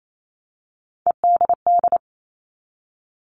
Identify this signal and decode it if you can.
Morse code, heard as a keyed tone, about 28 wpm, 700 Hz. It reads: EDD
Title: All Cut Numbers - 3-Digit 28wpm